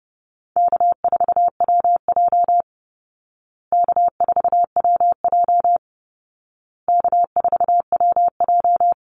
Morse code, heard as a keyed tone, about 30 wpm, 700 Hz. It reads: K4WJ K4WJ K4WJ
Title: Easy Call Signs Repeated 3x - 30wpm